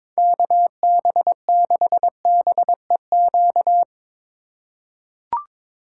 Morse code, heard as a keyed tone, about 22 wpm, 700 Hz. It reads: KB6BEQ
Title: USA Call Signs - 6 Characters - 4x Character Spacing - 22wpm